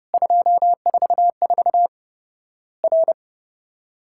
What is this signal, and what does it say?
Morse code, heard as a keyed tone, about 30 wpm, 700 Hz. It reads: 244 R
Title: Worked All Europe DX Contest - QTC Exchange - 30wpm